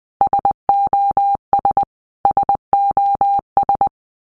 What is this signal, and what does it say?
Morse code, heard as a keyed tone, about 20 wpm, 800 Hz. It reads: SOS SOS
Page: SOS sound in Morse Code (Medium)